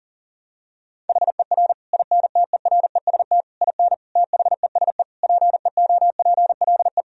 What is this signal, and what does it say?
Morse code, heard as a keyed tone, about 40 wpm, 700 Hz. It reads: HER INTEREST IN THESE PEOPLE
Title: Sentences from Top 1000 Words - Review Part 2 - 40wpm